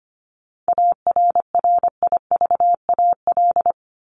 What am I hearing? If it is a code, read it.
Morse code, heard as a keyed tone, about 25 wpm, 700 Hz. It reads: ARRIVAL
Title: Top 1751-2000 Words 25wpm